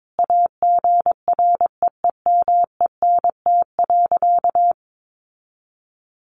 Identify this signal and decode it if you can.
Morse code, heard as a keyed tone, about 22 wpm, 700 Hz. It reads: AGREEMENT.
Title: Sentences from Top 1500 Words 22wpm